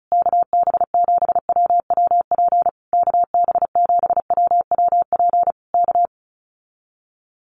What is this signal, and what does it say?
Morse code, heard as a keyed tone, about 34 wpm, 700 Hz. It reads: KB7WWP KB7WWP K
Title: Common QSO Phrases Encore 35wpm